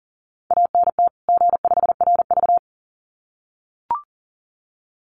Morse code, heard as a keyed tone, about 40 wpm, 700 Hz. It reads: ANT G5RV
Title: Common QSO Phrases 40wpm